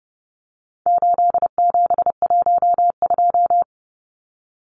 Morse code, heard as a keyed tone, about 30 wpm, 700 Hz. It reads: 8712
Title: Numbers - 4-Digit 30wpm